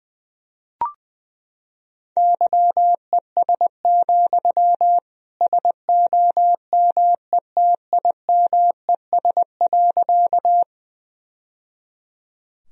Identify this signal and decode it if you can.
Morse code, heard as a keyed tone, about 20 wpm, 700 Hz. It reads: YES, SOMETIMES.
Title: Sentences from Top 1000 Words - Review Part 3 - 20wpm